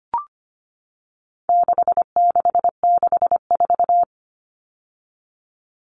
Morse code, heard as a keyed tone, about 25 wpm, 700 Hz. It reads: 6664
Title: Numbers - 4-Digit 25wpm